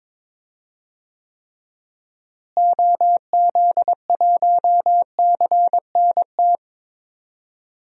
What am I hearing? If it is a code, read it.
Morse code, heard as a keyed tone, about 21 wpm, 700 Hz. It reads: OZ1CNT
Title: DXCC-100 Call Signs - 6 Characters - Repeated 2x - 22wpm